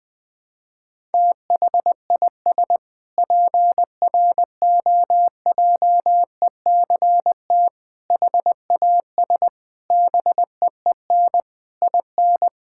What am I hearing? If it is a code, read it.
Morse code, heard as a keyed tone, about 21 wpm, 700 Hz. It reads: THIS PROJECT HAS BEEN IN
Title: Sentences with 6-Word Phrases - Spoken Beginning - 20wpm